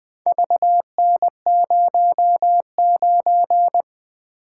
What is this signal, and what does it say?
Morse code, heard as a keyed tone, about 20 wpm, 700 Hz. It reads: VN09
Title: Four Letter-Number Combinations - 20wpm